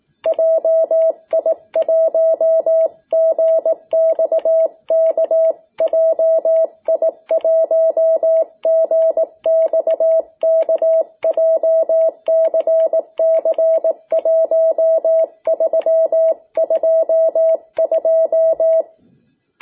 Morse code, heard as a keyed tone, about 18 wpm, 660 Hz. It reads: JI1GXK JI1GXK JCC1322
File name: jh8cbh / 2021cw